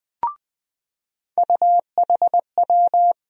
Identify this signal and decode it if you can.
Morse code, heard as a keyed tone, about 20 wpm, 700 Hz. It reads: UHW
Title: USA Call Sign Postfixes - 3 Letters 20wpm